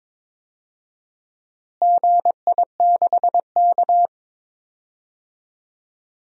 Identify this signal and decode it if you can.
Morse code, heard as a keyed tone, about 22 wpm, 700 Hz. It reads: GI6K
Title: DXCC-100 Call Signs - 4 Characters - Repeated 3x - 22wpm